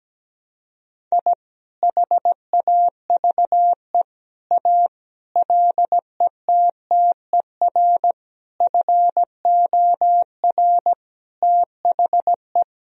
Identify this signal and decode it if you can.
Morse code, heard as a keyed tone, about 17 wpm, 700 Hz. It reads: I HAVE A LETTER FOR THE
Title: Sentences from Top 1000 Words - Review Part 1 - 17wpm